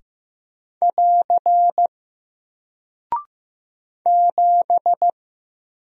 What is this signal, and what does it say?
Morse code, heard as a keyed tone, about 15 wpm, 700 Hz. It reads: <AR> 7
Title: Single Character (Letter:Number:Punctuation:Prosign) Alt 15wpm